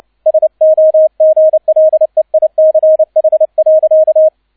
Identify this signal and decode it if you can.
Morse code, heard as a keyed tone, about 29 wpm, 630 Hz, with radio noise. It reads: SOGLEICH.